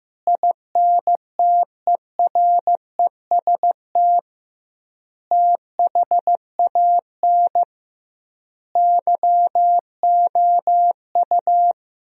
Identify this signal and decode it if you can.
Morse code, heard as a keyed tone, about 16 wpm, 700 Hz. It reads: INTEREST THAN YOU
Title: 2x Word-Spacing - Sentences from Top 400 Words - 15wpm